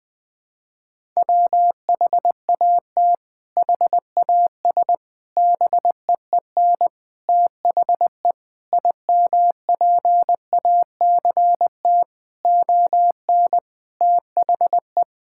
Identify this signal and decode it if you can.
Morse code, heard as a keyed tone, about 20 wpm, 700 Hz. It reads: WHAT HAS BEEN THE IMPACT ON THE
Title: Sentences from Top 1000 Words - Review Part 6 - 20wpm